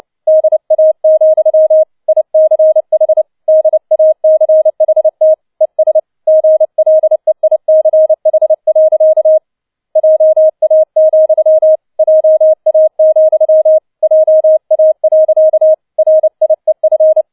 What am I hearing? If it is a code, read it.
Morse code, heard as a keyed tone, about 29 wpm, 630 Hz. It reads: DA, ICH DACHT ES GLEICH. JA, JA, JA. RIEF
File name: 007 Letzter Streich 30wpm